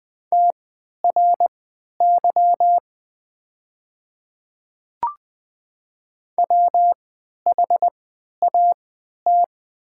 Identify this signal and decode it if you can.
Morse code, heard as a keyed tone, about 20 wpm, spaced as for 11 wpm, 700 Hz. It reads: TRY WHAT
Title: 3x Character Spacing - Top 200 Words - 20wpm